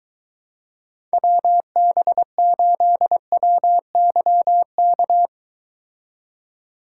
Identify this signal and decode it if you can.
Morse code, heard as a keyed tone, about 23 wpm, 700 Hz. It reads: WB8WYK